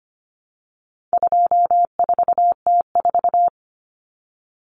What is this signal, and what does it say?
Morse code, heard as a keyed tone, about 25 wpm, 700 Hz. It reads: 24T4
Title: Common Cut Numbers - 4-Digit 25wpm